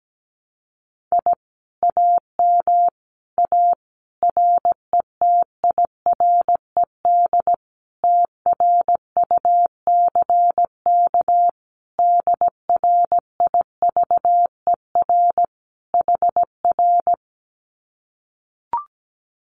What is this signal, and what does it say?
Morse code, heard as a keyed tone, about 17 wpm, 700 Hz. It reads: I AM A RETIRED TRUCK DRIVER HR
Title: Common QSO Phrases Encore 17wpm